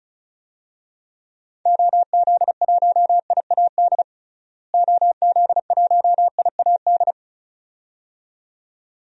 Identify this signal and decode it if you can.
Morse code, heard as a keyed tone, about 34 wpm, 700 Hz. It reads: OZ1IAD OZ1IAD
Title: DXCC-100 Call Signs - 6 Characters - Repeated 2x - 35wpm